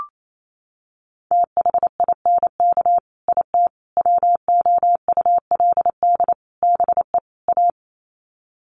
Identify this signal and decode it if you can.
Morse code, heard as a keyed tone, about 28 wpm, 700 Hz. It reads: THINK IT WOULD BE A
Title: Sets of 5 Words Encore2 28wpm